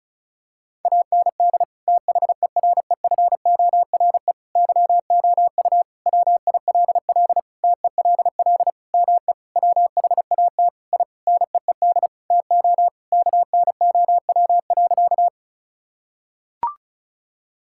Morse code, heard as a keyed tone, about 36 wpm, 700 Hz. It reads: AND THEREFORE YOU WILL TELL ME WHAT I NEED TO KNOW.
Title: Sentences from Top 1000 Words - Review Part 5 - 35wpm